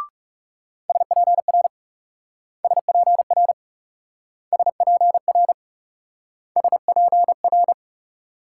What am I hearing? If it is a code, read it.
Morse code, heard as a keyed tone, about 37 wpm, 700 Hz. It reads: SPR SPR SPR SPR
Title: Speed-Racing Multi-letter Phonemes - 30wpm